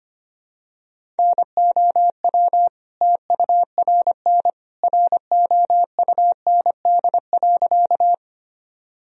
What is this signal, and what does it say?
Morse code, heard as a keyed tone, about 25 wpm, 700 Hz. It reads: NOW TURN ROUND.
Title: Sentences from Top 1000 Words - Review Part 1 - 25wpm